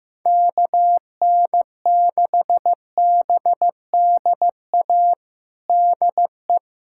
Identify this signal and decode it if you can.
Morse code, heard as a keyed tone, about 15 wpm, 700 Hz. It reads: KN6BDA DE
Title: USA Callsign DE Callsign 15wpm